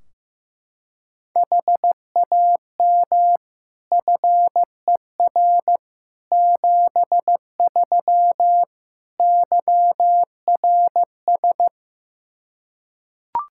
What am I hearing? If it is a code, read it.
Morse code, heard as a keyed tone, about 15 wpm, 700 Hz. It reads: HAM FER 73 YRS
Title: Practice QSO - Exchange 3 - More Info - 15wpm